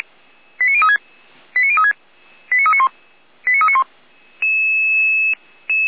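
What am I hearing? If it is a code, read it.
Five-tone selective call (ZVEI), heard as quick runs of five tones, tones of about 70 ms. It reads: 89036 89036 89391 89391
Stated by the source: ZVEI five-tone selective call 89502, received on an SDR.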